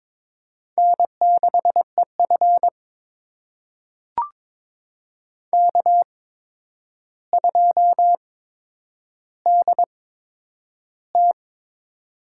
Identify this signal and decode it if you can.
Morse code, heard as a keyed tone, about 22 wpm, spaced as for 9 wpm, 700 Hz. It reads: N6EF K2DT
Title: USA Call Signs - 8x Character Spacing - 22wpm